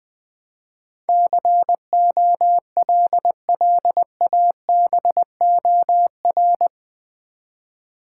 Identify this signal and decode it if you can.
Morse code, heard as a keyed tone, about 20 wpm, 700 Hz. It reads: COLLABOR
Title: Build A Word - 4 Letters At A Time - 11-Letter Words - 20wpm